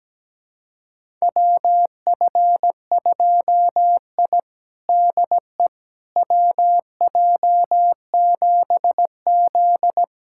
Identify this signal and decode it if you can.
Morse code, heard as a keyed tone, about 17 wpm, 700 Hz. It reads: WF2I DE WJ7Z
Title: USA Callsign DE Callsign - 4 Characters - 17wpm